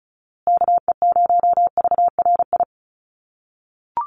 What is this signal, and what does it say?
Morse code, heard as a keyed tone, about 34 wpm, 700 Hz. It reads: KE0VRI
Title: USA Call Signs Repeated 2x 35wpm